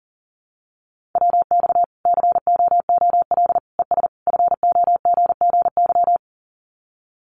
Morse code, heard as a keyed tone, about 40 wpm, 700 Hz. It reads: WX COOL ES FOGGY
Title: Common QSO Phrases 40wpm